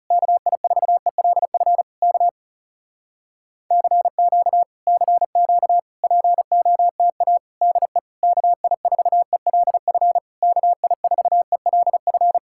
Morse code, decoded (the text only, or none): KI4ELF K CQ CQ POTA DE KI4ELF KI4ELF